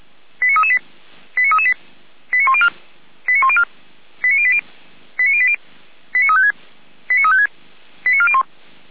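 Five-tone selective call (ZVEI), heard as quick runs of five tones, tones of about 70 ms. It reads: 89338 89338 89104 89104 89080 89080 89356 89356 89401
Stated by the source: ZVEI five-tone selective call received on an SDR.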